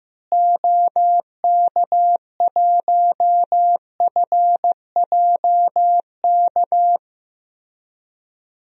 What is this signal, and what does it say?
Morse code, heard as a keyed tone, about 15 wpm, 700 Hz. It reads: OK1FJK
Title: DXCC-100 Call Signs - 6 Characters - Repeated 3x - 15wpm